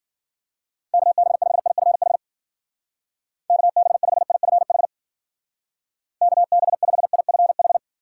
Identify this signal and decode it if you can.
Morse code, heard as a keyed tone, about 52 wpm, 700 Hz. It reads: KB5IFH KB5IFH KB5IFH